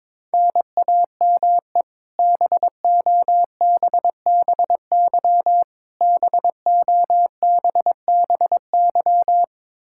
Morse code, heard as a keyed tone, about 22 wpm, 700 Hz. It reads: NAME BOBBY BOBBY
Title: Practice QSO - Exchange 1 - Beginning - 22wpm